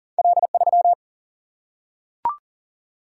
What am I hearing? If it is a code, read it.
Morse code, heard as a keyed tone, about 38 wpm, 700 Hz. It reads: L3
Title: Single-Letter Single-Number 40wpm